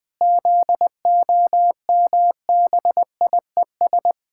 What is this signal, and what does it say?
Morse code, heard as a keyed tone, about 20 wpm, 700 Hz. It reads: ZOMBIES
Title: Pangram Challenge v2 - 20wpm